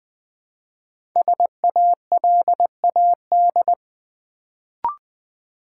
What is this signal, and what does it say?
Morse code, heard as a keyed tone, about 21 wpm, 700 Hz. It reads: SALAD
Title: Build A Word - 1 Letter At A Time - 3 to 5-Letter Words - 20wpm